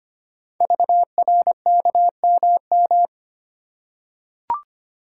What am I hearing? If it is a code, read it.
Morse code, heard as a keyed tone, about 25 wpm, 700 Hz. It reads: VRKMM
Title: Code Groups - 5 Letters - 25wpm